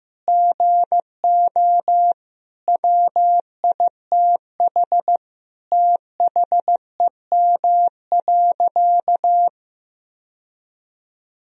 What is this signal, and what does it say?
Morse code, heard as a keyed tone, about 15 wpm, 700 Hz. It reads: GO WITH THEM.
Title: Sentences from Top 1000 Words - Review Part 4 - 15wpm